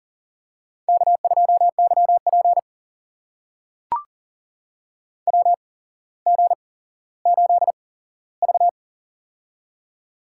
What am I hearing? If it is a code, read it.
Morse code, heard as a keyed tone, about 40 wpm, spaced as for 17 wpm, 700 Hz. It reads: K2YP WG8V
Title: USA Call Signs - 4 Characters - 8x Character Spacing - 40wpm